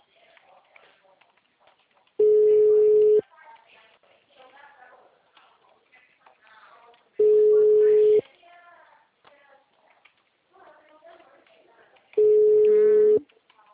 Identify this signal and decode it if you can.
Ringback tone (Europe, 425 Hz), with speech.